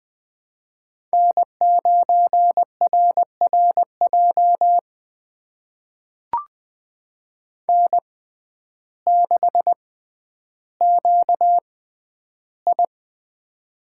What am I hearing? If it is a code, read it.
Morse code, heard as a keyed tone, about 20 wpm, spaced as for 11 wpm, 700 Hz. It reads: N9RRJ N6QI
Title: USA Call Signs - 6x Character Spacing - 20wpm